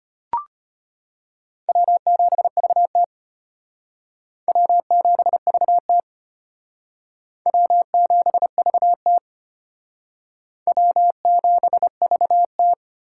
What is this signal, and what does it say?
Morse code, heard as a keyed tone, about 30 wpm, 700 Hz. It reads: W7VT W7VT W7VT W7VT
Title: Speed Racing - USA 1x2 Call Signs 25wpm